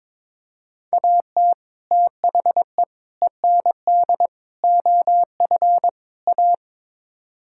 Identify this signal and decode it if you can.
Morse code, heard as a keyed tone, about 22 wpm, 700 Hz. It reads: AT THE END OF A
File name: practice / 22wpm / Build A Phrase - 5-Word Phrases - 22wpm